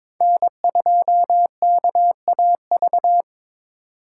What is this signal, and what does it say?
Morse code, heard as a keyed tone, about 22 wpm, 700 Hz. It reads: N2KAV
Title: USA Call Signs - 5 Characters - 4x Character Spacing - 22wpm